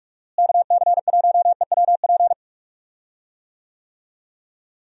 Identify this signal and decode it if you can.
Morse code, heard as a keyed tone, about 48 wpm, 700 Hz. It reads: KC1EWP
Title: USA Call Signs - 6 Characters - Repeated 2x - 45wpm